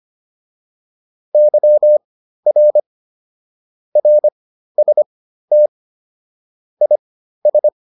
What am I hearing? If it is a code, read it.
Morse code, heard as a keyed tone, about 25 wpm, spaced as for 13 wpm, 600 Hz. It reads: YR RST IS
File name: QSO304 13